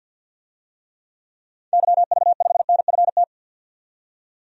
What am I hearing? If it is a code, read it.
Morse code, heard as a keyed tone, about 48 wpm, 700 Hz. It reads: YV5NFT
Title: DXCC-100 Call Signs - 6 Characters - 50wpm